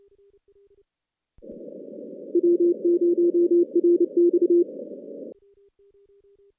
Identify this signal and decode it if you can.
Morse code, heard as a keyed tone, about 29 wpm, 350 Hz. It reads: W0RX
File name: 2016 JAN CW NAQP / 2016 Jan naqp k5ot - 0-10